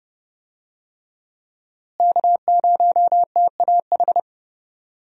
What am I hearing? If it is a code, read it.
Morse code, heard as a keyed tone, about 30 wpm, 700 Hz. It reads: K0TAH
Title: USA Call Signs - 5 Characters - Repeated 3x - 30wpm